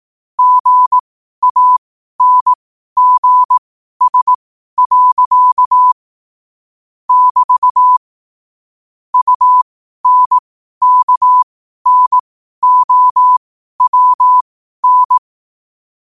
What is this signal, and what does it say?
Morse code, heard as a keyed tone, about 18 wpm, spaced as for 12 wpm, 1000 Hz. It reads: GANGS. <BT> UNKNOWN
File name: qotdfiles / 12wpm